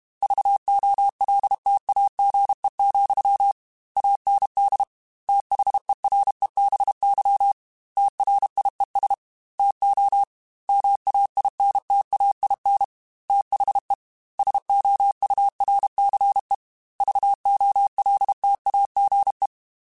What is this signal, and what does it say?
Morse code, heard as a keyed tone, about 32 wpm, 800 Hz. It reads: VOLTAGE, AND THEREBY TRIES TO MAINTAIN THE SOURCE VOLTAGE